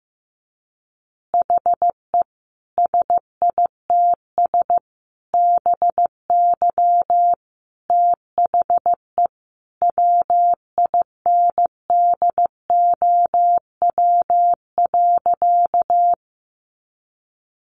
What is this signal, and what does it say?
Morse code, heard as a keyed tone, about 15 wpm, 700 Hz. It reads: HE SITS BY THE WINDOW.
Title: Additional Easy Phrases 15wpm